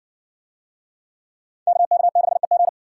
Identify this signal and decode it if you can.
Morse code, heard as a keyed tone, about 58 wpm, 700 Hz. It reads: KK6EG